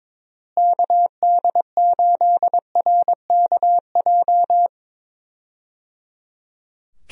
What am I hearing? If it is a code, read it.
Morse code, heard as a keyed tone, about 22 wpm, 700 Hz. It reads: KD8RKJ
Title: USA Call Signs - 6 Characters - Repeated 3x - 22wpm